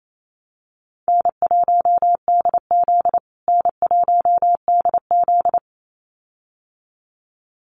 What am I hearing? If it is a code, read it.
Morse code, heard as a keyed tone, about 28 wpm, 700 Hz. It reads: N1DZ N1DZ
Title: USA Call Signs Repeated 2x 28wpm